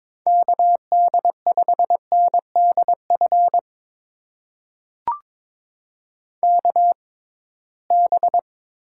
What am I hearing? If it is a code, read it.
Morse code, heard as a keyed tone, about 22 wpm, spaced as for 17 wpm, 700 Hz. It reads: KD5NDF KB
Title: USA Call Signs - 6 Characters - 6x Character Spacing - 22wpm